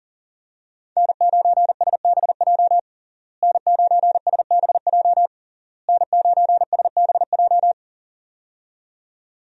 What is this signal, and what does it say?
Morse code, heard as a keyed tone, about 40 wpm, 700 Hz. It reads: N9SBJ N9SBJ N9SBJ
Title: USA Call Signs - 5 Characters - Repeated 3x - 40wpm